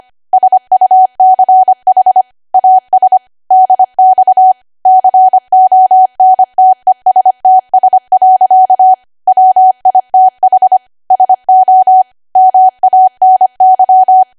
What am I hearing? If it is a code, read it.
Morse code, heard as a keyed tone, about 25 wpm, 750 Hz. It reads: SUCH AS DX CONTESTS. WITH SO MANY
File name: cwa / W1AW-Level3 / 25WPM